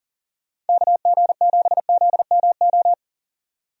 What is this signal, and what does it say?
Morse code, heard as a keyed tone, about 40 wpm, 700 Hz. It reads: KG7ZMO
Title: USA Call Signs - 6 Characters - 40wpm